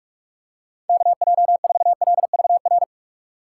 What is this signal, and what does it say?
Morse code, heard as a keyed tone, about 46 wpm, 700 Hz. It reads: KJ4LVR
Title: USA Call Signs - 6x Character Spacing - 45wpm